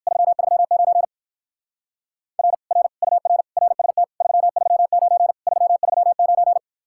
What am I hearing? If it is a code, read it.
Morse code, heard as a keyed tone, about 58 wpm, 700 Hz. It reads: 339 R R UR RST 339 339